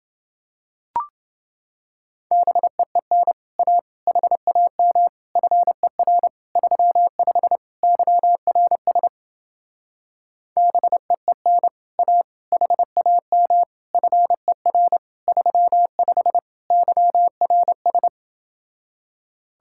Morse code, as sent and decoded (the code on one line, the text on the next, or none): -... . -. / .- / .... .- -- / ..-. .-. / ...-- ..... / -.-- .-. ... / -... . -. / .- / .... .- -- / ..-. .-. / ...-- ..... / -.-- .-. ...
BEEN A HAM FER 35 YRS BEEN A HAM FER 35 YRS